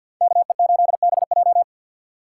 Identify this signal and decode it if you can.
Morse code, heard as a keyed tone, about 62 wpm, 700 Hz. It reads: KE8BJ
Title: Speed Racing - USA 2x2 Call Signs 50wpm